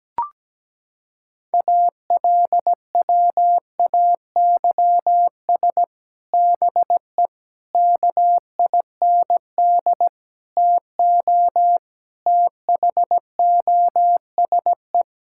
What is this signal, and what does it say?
Morse code, heard as a keyed tone, about 17 wpm, 700 Hz. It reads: ALWAYS BE KIND TO THOSE
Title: Sentences from Top 1600 Words 17wpm